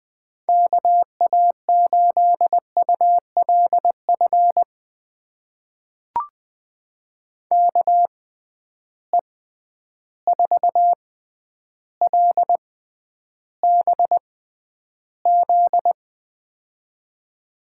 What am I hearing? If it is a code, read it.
Morse code, heard as a keyed tone, about 20 wpm, spaced as for 10 wpm, 700 Hz. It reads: KA8ULF KE4LBZ
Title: USA Call Signs - 6 Characters - 6x Character Spacing - 20wpm